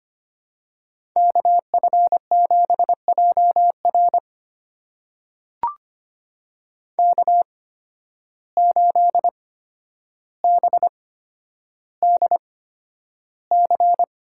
Morse code, heard as a keyed tone, about 25 wpm, spaced as for 11 wpm, 700 Hz. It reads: KF7JR K8BDC
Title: USA Call Signs - 5 Characters - 8x Character Spacing - 25wpm